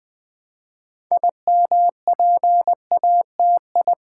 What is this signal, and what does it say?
Morse code, heard as a keyed tone, about 20 wpm, 700 Hz. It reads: IMPATI
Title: Build A Word - 3 Letters At A Time - 9-Letter Words - 20wpm